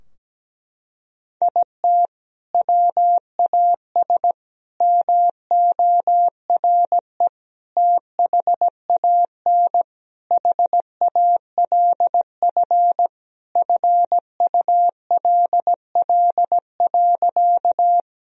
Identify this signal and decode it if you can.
Morse code, heard as a keyed tone, about 17 wpm, 700 Hz. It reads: IT WAS MORE THAN HALF FULL.
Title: Sentences from Top 1000 Words - Review Part 3 - 17wpm